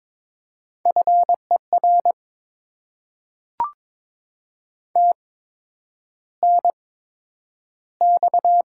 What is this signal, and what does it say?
Morse code, heard as a keyed tone, about 22 wpm, spaced as for 9 wpm, 700 Hz. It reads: FER TNX
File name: Top 20 QSO Elements - 8x Character Spacing - 22wpm